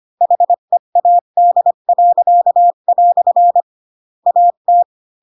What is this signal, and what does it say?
Morse code, heard as a keyed tone, about 25 wpm, 700 Hz. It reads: HEAD." AT
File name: Pinocchio / Pinocchio ch4 20 25 ik1hge